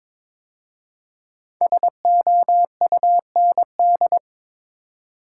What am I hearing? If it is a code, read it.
Morse code, heard as a keyed tone, about 22 wpm, 700 Hz. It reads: SOUND
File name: 5 Letter Words - First 3 Letters Given - 22wpm